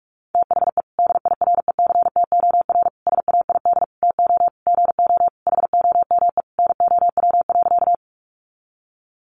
Morse code, heard as a keyed tone, about 44 wpm, 700 Hz. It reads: THE DIRECTOR SAID TO GO HOME NOW.
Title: Sentences from Top 1300 Words 45wpm